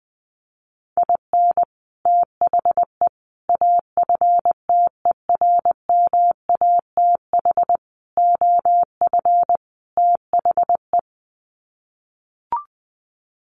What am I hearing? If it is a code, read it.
Morse code, heard as a keyed tone, about 20 wpm, 700 Hz. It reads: IN THE AFTERMATH OF THE
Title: Sets of 5 Words 20wpm